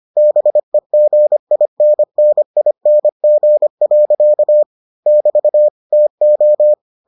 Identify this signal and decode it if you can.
Morse code, heard as a keyed tone, about 25 wpm, 600 Hz. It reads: BEGINNING. <BT> TO